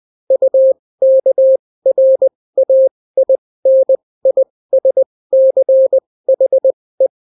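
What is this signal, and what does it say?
Morse code, heard as a keyed tone, about 20 wpm, spaced as for 16 wpm, 530 Hz. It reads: UKRAINISCHE
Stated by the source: Daily Morse weather and top stories at 16 wpm, clean without QRN.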